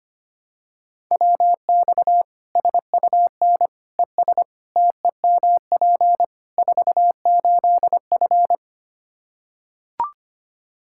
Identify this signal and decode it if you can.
Morse code, heard as a keyed tone, about 25 wpm, 700 Hz. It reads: WX SUN ES TEMP 48F
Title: K1USN SST QSO Practice 25wpm